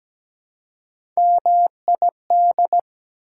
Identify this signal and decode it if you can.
Morse code, heard as a keyed tone, about 17 wpm, 700 Hz. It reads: MID
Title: Prefixes and Suffixes 17wpm